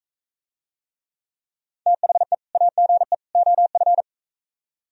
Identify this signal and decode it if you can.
Morse code, heard as a keyed tone, about 43 wpm, 700 Hz. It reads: THE AGE OF